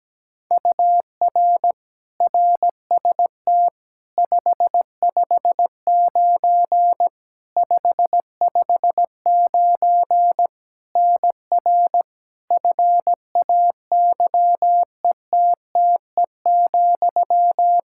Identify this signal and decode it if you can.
Morse code, heard as a keyed tone, about 17 wpm, 700 Hz. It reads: UR RST 559 559 NR FAYETTE,
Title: Practice QSO - Exchange 1 - Beginning - 17wpm